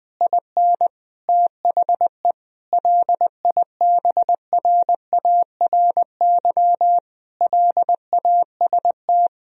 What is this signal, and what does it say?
Morse code, heard as a keyed tone, about 20 wpm, 700 Hz. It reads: IN THE LIBRARY LAST
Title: Sentences from Top 1000 Words - Review Part 4 - 20wpm